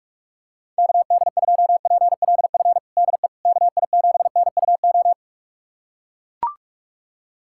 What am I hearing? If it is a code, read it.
Morse code, heard as a keyed tone, about 42 wpm, 700 Hz. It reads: KD2PLF DE KI7NUO
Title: USA Callsign DE Callsign - 6 Characters - 45wpm